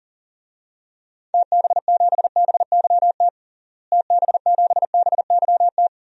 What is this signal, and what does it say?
Morse code, heard as a keyed tone, about 40 wpm, 700 Hz. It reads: TB7BYT TB7BYT
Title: DXCC-100 Call Signs - 6 Characters - Repeated 3x - 40wpm